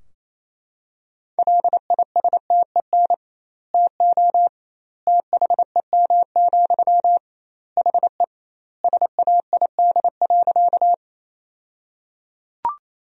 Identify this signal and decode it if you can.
Morse code, heard as a keyed tone, about 28 wpm, 700 Hz. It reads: LISTEN TO THEM, HE SAID.
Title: 2x Word-Spacing - Sentences from Top 400 Words - 28wpm